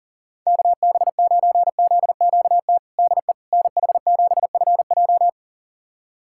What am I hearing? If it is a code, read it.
Morse code, heard as a keyed tone, about 37 wpm, 700 Hz. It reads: KB9ZQT DE NH7FJ
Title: USA Callsign DE Callsign 40wpm